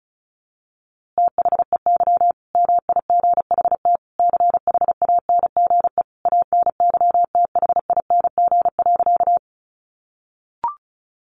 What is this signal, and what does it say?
Morse code, heard as a keyed tone, about 34 wpm, 700 Hz. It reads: THEY MIGHT CHANGE ANYTHING.
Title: A Sound Of Thunder 35wpm